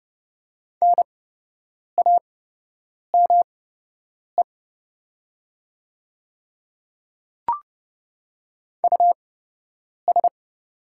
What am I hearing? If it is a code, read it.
Morse code, heard as a keyed tone, about 29 wpm, spaced as for 8 wpm, 700 Hz. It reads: NAME US
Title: 8x Character Spacing - Top 200 Words - 30wpm